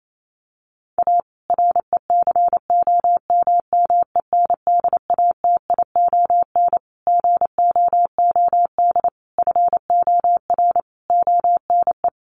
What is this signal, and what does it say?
Morse code, heard as a keyed tone, about 28 wpm, 700 Hz. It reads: A RECOMMENDATION GOOD FOR ONE
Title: Sentences from Top 1400 Words 28wpm